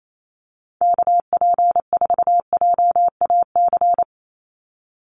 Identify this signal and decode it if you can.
Morse code, heard as a keyed tone, about 28 wpm, 700 Hz. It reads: KP4JAC
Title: USA Call Signs 28wpm